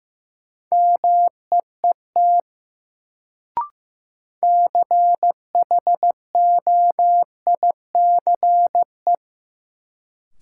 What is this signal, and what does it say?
Morse code, heard as a keyed tone, about 15 wpm, 700 Hz. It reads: MEET CHOICE
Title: Sets of 5 Words - New Word Drill - 15wpm